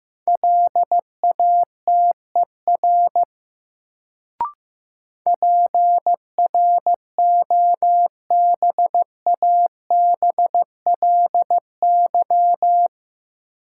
Morse code, heard as a keyed tone, about 15 wpm, 700 Hz. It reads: LATER PROBABLY
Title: Sentences from Top 700 Words - New Word Drill - 15wpm